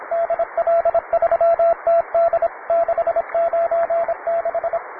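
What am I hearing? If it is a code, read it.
Morse code, heard as a keyed tone, about 26 wpm, 670 Hz, with radio noise. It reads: DL3TD 696
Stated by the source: Morse code QTC list traffic from the Worked All Europe (WAE) contest, advanced level.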